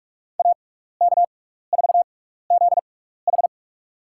Morse code, heard as a keyed tone, about 44 wpm, spaced as for 15 wpm, 700 Hz. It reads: AK4ZH
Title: USA Call Signs - 5 Characters - 6x Character Spacing - 45wpm